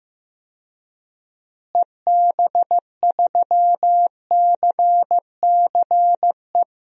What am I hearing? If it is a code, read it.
Morse code, heard as a keyed tone, about 15 wpm, 700 Hz. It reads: EB3CCE
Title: DXCC-100 Call Signs - 6 Characters - Repeated 3x - 15wpm